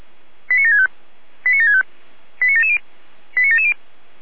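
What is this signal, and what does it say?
Five-tone selective call (ZVEI), heard as quick runs of five tones, tones of about 80 ms. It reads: 89765 89765 89770 89770
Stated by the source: ZVEI five-tone selective call received with an SDR.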